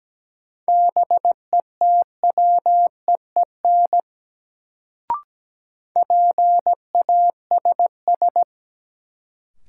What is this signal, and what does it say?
Morse code, heard as a keyed tone, about 17 wpm, 700 Hz. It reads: BETWEEN PASS
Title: Sentences from Top 400 Words - New Word Drill - 17wpm